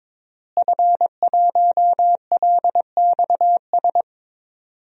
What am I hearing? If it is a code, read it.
Morse code, heard as a keyed tone, about 22 wpm, 700 Hz. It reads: F1LXS
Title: DXCC-100 Call Signs - 5 Characters - Repeated 3x - 22wpm